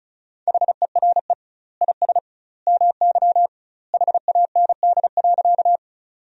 Morse code, heard as a keyed tone, about 36 wpm, 700 Hz. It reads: HERE IS MY HAND.
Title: 2x Word-Spacing - Sentences from Top 200 Words - 35wpm